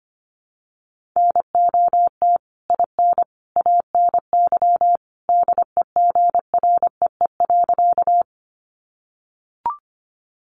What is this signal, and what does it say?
Morse code, heard as a keyed tone, about 25 wpm, 700 Hz. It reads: NOT IN ANY DEGREE.
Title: Sentences from Top 1500 Words 25wpm